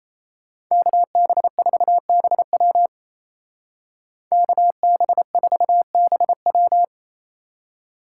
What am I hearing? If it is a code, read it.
Morse code, heard as a keyed tone, about 35 wpm, 700 Hz. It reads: KB4BW KB4BW
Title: Speed Racing - USA 2x2 Call Signs 28wpm